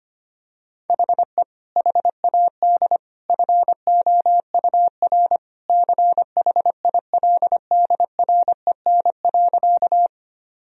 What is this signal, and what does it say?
Morse code, heard as a keyed tone, about 25 wpm, 700 Hz. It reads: HE HAD FOUR CHILDREN.
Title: Sentences from Top 1000 Words - Review Part 1 - 25wpm